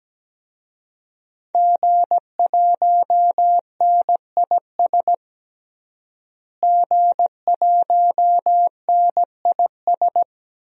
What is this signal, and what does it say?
Morse code, heard as a keyed tone, about 17 wpm, 700 Hz. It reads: G1NIS G1NIS
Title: DXCC-100 Call Signs - 5 Characters - Repeated 3x - 17wpm